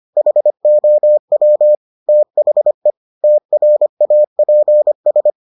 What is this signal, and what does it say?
Morse code, heard as a keyed tone, about 25 wpm, 600 Hz. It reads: HOW THE TRAPS